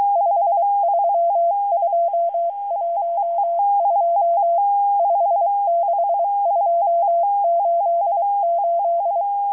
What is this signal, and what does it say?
Morse code, heard as a keyed tone, about 23 wpm, 680 Hz. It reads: 53212 56288